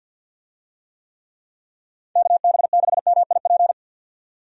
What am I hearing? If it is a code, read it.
Morse code, heard as a keyed tone, about 52 wpm, 700 Hz. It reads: KB6MIP
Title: USA Call Signs - 6 Characters - 50wpm